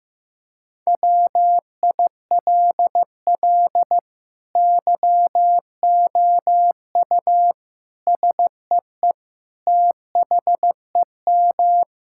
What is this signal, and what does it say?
Morse code, heard as a keyed tone, about 15 wpm, 700 Hz. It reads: WILL YOU SEE THEM
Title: Sentences from Top 1600 Words 15wpm